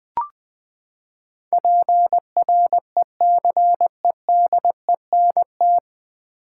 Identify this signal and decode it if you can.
Morse code, heard as a keyed tone, about 20 wpm, 700 Hz. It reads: PRECEDENT
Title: Nine Letter Words - 20wpm